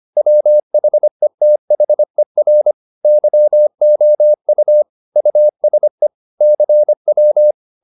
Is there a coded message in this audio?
Yes: Morse code from 0.2 to 7.5 s, heard as a keyed tone, about 25 wpm, 600 Hz. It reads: WHETHER YOU USE CW